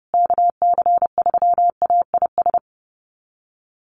Morse code, heard as a keyed tone, about 30 wpm, 700 Hz. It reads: KC3AIS
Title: USA Call Signs Repeated 3x 30wpm